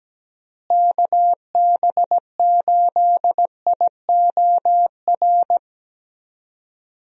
Morse code, heard as a keyed tone, about 17 wpm, 700 Hz. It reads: KB8IOR
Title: USA Call Signs - 6 Characters - Repeated 3x - 17wpm